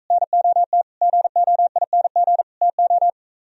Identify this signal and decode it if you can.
Morse code, heard as a keyed tone, about 45 wpm, 700 Hz. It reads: NOT GOING TO